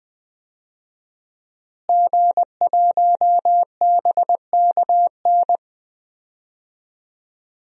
Morse code, heard as a keyed tone, about 20 wpm, 700 Hz. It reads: G1BKN
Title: DXCC-100 Call Signs - 5 Characters - Repeated 3x - 20wpm